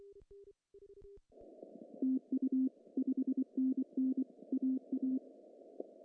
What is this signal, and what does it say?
Morse code, heard as a keyed tone, about 24 wpm, 260 Hz. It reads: TU 5NN AA
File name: CW / cqww-cw-k5tr